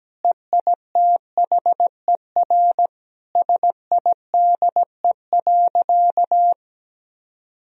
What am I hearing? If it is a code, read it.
Morse code, heard as a keyed tone, about 17 wpm, 700 Hz. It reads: EITHER SIDE.
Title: Sentences from Top 1000 Words - Review Part 4 - 17wpm